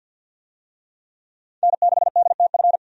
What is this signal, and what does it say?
Morse code, heard as a keyed tone, about 50 wpm, 700 Hz. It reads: N6DTF
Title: USA Call Signs - 5 Characters - 50wpm